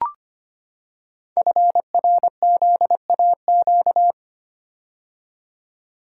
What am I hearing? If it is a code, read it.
Morse code, heard as a keyed tone, about 25 wpm, 700 Hz. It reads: FRZAQ
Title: Code Groups - 5 Letters - 25wpm